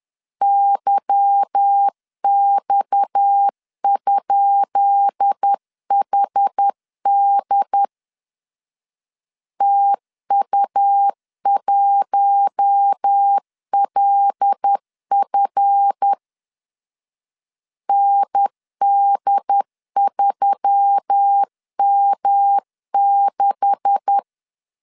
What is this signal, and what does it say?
Morse code, heard as a keyed tone, about 10 wpm, 800 Hz. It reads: YX?HD TU1LF ND3M6
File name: LEZIONI / 10 SIGNS / 10B INT 10wpm 2844